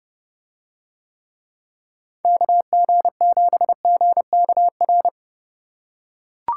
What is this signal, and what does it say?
Morse code, heard as a keyed tone, about 31 wpm, 700 Hz. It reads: KG7GKR